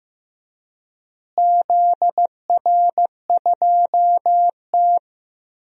Morse code, heard as a keyed tone, about 15 wpm, 700 Hz. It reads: ZR2T
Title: DXCC-100 Call Signs - 4 Characters - Repeated 3x - 15wpm